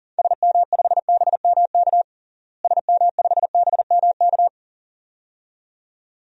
Morse code, heard as a keyed tone, about 39 wpm, 700 Hz. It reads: SM5BMK SM5BMK
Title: DXCC-100 Call Signs - 6 Characters - Repeated 2x - 40wpm